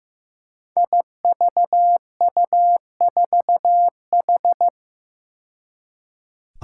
Morse code, heard as a keyed tone, about 15 wpm, 700 Hz. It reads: IVU4H
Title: Five Challenging Letter-Number Combinations 15wpm